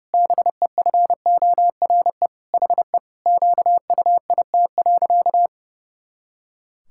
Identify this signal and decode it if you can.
Morse code, heard as a keyed tone, about 28 wpm, 700 Hz. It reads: BEFORE HE QUIT.
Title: Pangram Challenge v2 - 30wpm